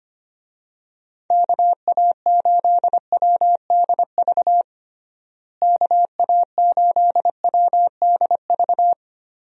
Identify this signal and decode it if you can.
Morse code, heard as a keyed tone, about 25 wpm, 700 Hz. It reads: KA8WDV KA8WDV
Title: USA Call Signs - 6 Characters - Repeated 3x - 25wpm